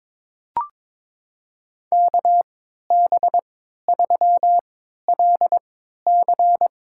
Morse code, heard as a keyed tone, about 22 wpm, spaced as for 13 wpm, 700 Hz. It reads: KB3LC
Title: USA Call Signs - 3x Character Spacing - 22wpm